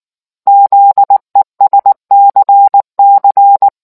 Morse code, heard as a keyed tone, about 19 wpm, 800 Hz. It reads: ZESCC